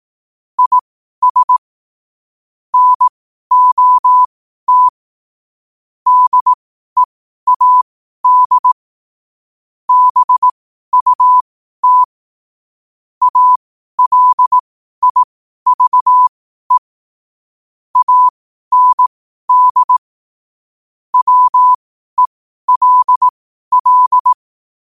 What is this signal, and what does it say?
Morse code, heard as a keyed tone, about 18 wpm, spaced as for 12 wpm, 1000 Hz. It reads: IS NOT DEAD BUT ALIVE AND WELL
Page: Quotes for Thu, 14 Aug 2025 in Morse Code at 12 words per minute.